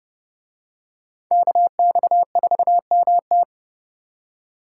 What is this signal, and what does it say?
Morse code, heard as a keyed tone, about 29 wpm, 700 Hz. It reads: KX4MT